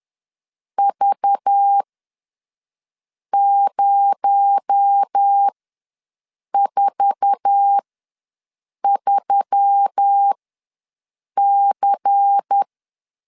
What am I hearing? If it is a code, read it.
Morse code, heard as a keyed tone, about 10 wpm, spaced as for 5 wpm, 800 Hz. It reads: V043C
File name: LEZIONI / 8 YJ45 / 8 YJ4 ws 10wpm 4020